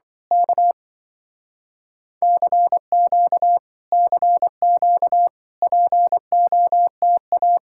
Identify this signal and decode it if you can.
Morse code, heard as a keyed tone, about 24 wpm, 700 Hz. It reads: K CQ CQ POTA
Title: Speed-Racing Common QSO Phrases Encore - 20wpm